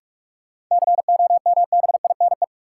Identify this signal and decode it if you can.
Morse code, heard as a keyed tone, about 42 wpm, 700 Hz. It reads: COMBINE
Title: Sentences from Top 1600 Words - New Word Drill Alt - 45wpm